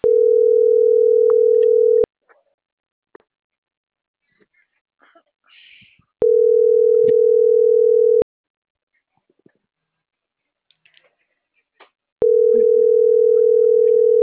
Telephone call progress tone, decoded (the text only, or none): ringback tone